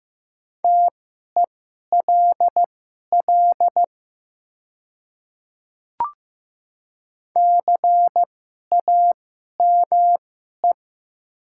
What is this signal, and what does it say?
Morse code, heard as a keyed tone, about 15 wpm, spaced as for 11 wpm, 700 Hz. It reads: TELL CAME
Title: 2x Character Spacing - Top 200 Words - 15wpm